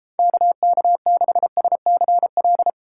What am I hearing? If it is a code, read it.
Morse code, heard as a keyed tone, about 33 wpm, 700 Hz. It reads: KK6SCL